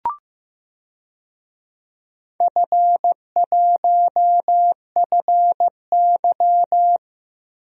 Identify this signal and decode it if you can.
Morse code, heard as a keyed tone, about 16 wpm, 700 Hz. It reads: F1FY
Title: DXCC-100 Call Signs - 4 Characters - Repeated 3x - 15wpm